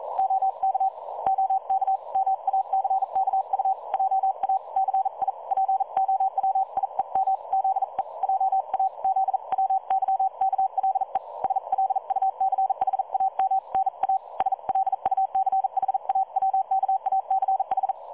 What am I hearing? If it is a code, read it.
Morse code, heard as a keyed tone, about 42 wpm, 770 Hz. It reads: OK OK MACKU JA CI POWIEM ZE JA ZWYKLE SLUCHAM NA SLUCHAWKACH
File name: dlazaawansowanych / nagraniazpasma / websdr recording 3565.0kHz